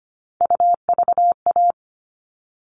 Morse code, heard as a keyed tone, about 25 wpm, 700 Hz. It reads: UVA
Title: All Cut Numbers - 3-Digit 25wpm